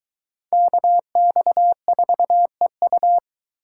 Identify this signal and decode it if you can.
Morse code, heard as a keyed tone, about 20 wpm, 700 Hz. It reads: KX4EU